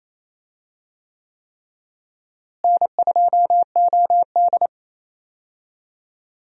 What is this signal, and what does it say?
Morse code, heard as a keyed tone, about 28 wpm, 700 Hz. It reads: N2OD
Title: USA Call Signs - 4 Characters - Repeated 3x - 28wpm